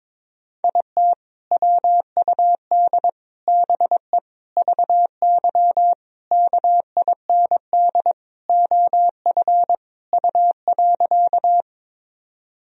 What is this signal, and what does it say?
Morse code, heard as a keyed tone, about 22 wpm, 700 Hz. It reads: IT WUD BE VY KIND OF U.
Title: Sentences from Top 200 Words and Abbreviations 22wpm